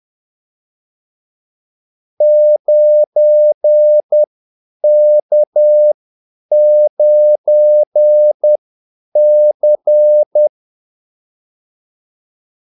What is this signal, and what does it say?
Morse code, heard as a keyed tone, about 10 wpm, spaced as for 8 wpm, 600 Hz. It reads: 9K9C
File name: Callsigns-10-8wpm